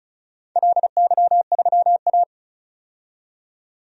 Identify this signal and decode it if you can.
Morse code, heard as a keyed tone, about 36 wpm, 700 Hz. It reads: LY3A
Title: DXCC-100 Call Signs - 4 Characters - 35wpm